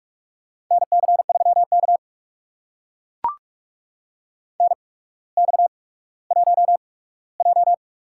Morse code, heard as a keyed tone, about 44 wpm, spaced as for 19 wpm, 700 Hz. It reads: NC3K NX1J
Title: USA Call Signs - 4 Characters - 8x Character Spacing - 45wpm